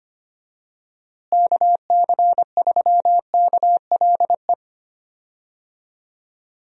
Morse code, heard as a keyed tone, about 25 wpm, 700 Hz. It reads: KC3KLE
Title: USA Call Signs - 6 Characters - Repeated 3x - 25wpm